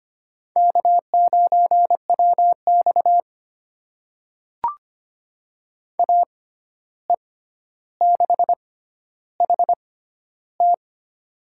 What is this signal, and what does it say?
Morse code, heard as a keyed tone, about 25 wpm, spaced as for 12 wpm, 700 Hz. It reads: K9WX AE6HT
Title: USA Call Signs - 6x Character Spacing - 25wpm